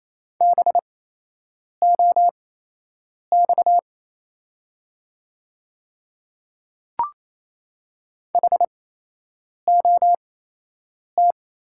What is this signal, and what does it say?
Morse code, heard as a keyed tone, about 28 wpm, spaced as for 8 wpm, 700 Hz. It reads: BOX HOT
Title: 8x Character Spacing - Top 500 Words - 28wpm